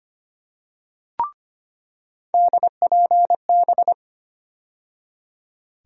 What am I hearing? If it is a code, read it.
Morse code, heard as a keyed tone, about 25 wpm, 700 Hz. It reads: DPB
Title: USA Call Sign Postfixes - 3 Letters 25wpm